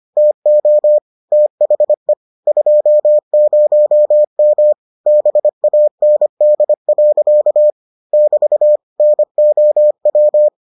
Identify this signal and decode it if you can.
Morse code, heard as a keyed tone, about 25 wpm, 600 Hz. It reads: TO THE 20M BAND. <BT> NOW